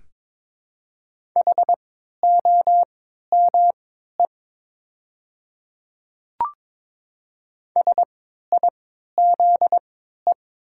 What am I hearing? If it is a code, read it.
Morse code, heard as a keyed tone, about 22 wpm, spaced as for 12 wpm, 700 Hz. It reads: HOME SIZE
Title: 3x Character Spacing - Top 500 Words - 22wpm